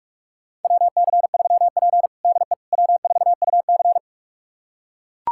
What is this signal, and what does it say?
Morse code, heard as a keyed tone, about 44 wpm, 690 Hz. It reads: WC3P DE W4UC
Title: USA Callsign DE Callsign - 4 Characters - 45wpm